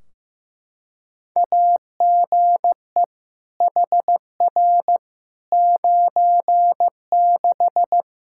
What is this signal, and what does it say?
Morse code, heard as a keyed tone, about 15 wpm, 700 Hz. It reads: AGE HR 96
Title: Practice QSO - Exchange 3 - More Info - 15wpm